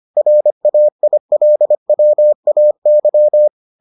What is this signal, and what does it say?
Morse code, heard as a keyed tone, about 25 wpm, 600 Hz. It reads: RAILWAY